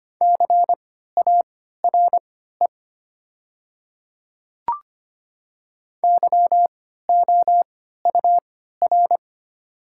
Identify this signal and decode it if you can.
Morse code, heard as a keyed tone, about 25 wpm, spaced as for 14 wpm, 700 Hz. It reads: CARE YOUR
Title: 3x Character Spacing - Top 300 Words - 25wpm